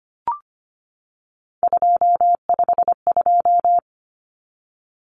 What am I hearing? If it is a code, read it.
Morse code, heard as a keyed tone, about 25 wpm, 700 Hz. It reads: 252
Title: Common Cut Numbers - 3-Digit 25wpm